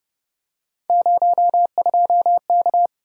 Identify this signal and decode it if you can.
Morse code, heard as a keyed tone, about 27 wpm, 700 Hz. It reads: 02K